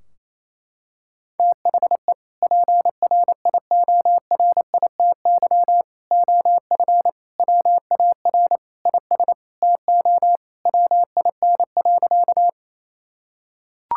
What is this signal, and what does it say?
Morse code, heard as a keyed tone, about 28 wpm, 700 Hz. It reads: THE PRIORITY OF WAR IS TO WIN.
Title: Sentences from Top 1600 Words 28wpm